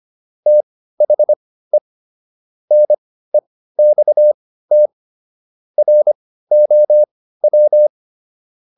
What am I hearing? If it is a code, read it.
Morse code, heard as a keyed tone, about 25 wpm, spaced as for 15 wpm, 600 Hz. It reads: THE NEXT ROW